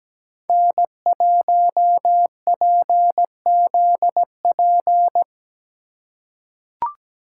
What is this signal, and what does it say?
Morse code, heard as a keyed tone, about 17 wpm, 700 Hz. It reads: N1PZP
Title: USA Call Signs - 5 Characters - 4x Character Spacing - 17wpm